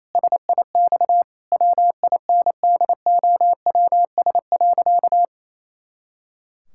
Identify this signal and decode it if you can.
Morse code, heard as a keyed tone, about 28 wpm, 700 Hz. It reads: SIX WINDOWS.
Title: Pangram Challenge v2 - 28wpm